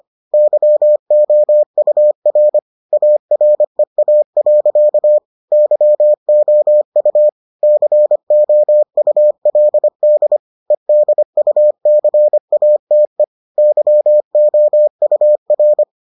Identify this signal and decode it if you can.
Morse code, heard as a keyed tone, about 25 wpm, 600 Hz. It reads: YOUR AREA. YOU COULD EDUCATE YOUR